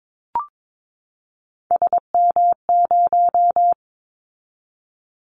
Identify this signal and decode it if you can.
Morse code, heard as a keyed tone, about 22 wpm, 700 Hz. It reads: SM0
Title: Callsign Prefixes for Top 25 DXCC - 22wpm